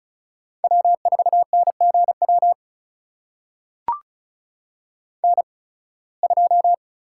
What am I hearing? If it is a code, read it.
Morse code, heard as a keyed tone, about 36 wpm, spaced as for 23 wpm, 700 Hz. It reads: W4NGW N2
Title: USA Call Signs - 5 Characters - 8x Character Spacing - 35wpm